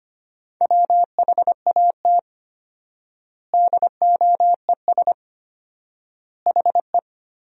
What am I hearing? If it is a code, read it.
Morse code, heard as a keyed tone, about 25 wpm, 700 Hz. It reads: WHAT DOES HE
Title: 4x Word-Spacing - Sentences from Top 400 Words - 25wpm